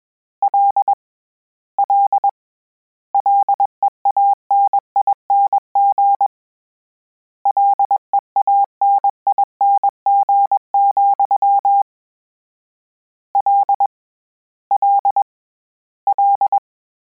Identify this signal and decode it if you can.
Morse code, heard as a keyed tone, about 21 wpm, 800 Hz. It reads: L L LEANING LEANING, L L L
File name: Combinations / Combinations-mp3s / Combinations@20